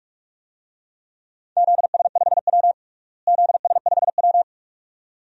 Morse code, heard as a keyed tone, about 41 wpm, 700 Hz. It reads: ZS5W ZS5W
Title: DXCC-100 Call Signs - 4 Characters - Repeated 2x - 45wpm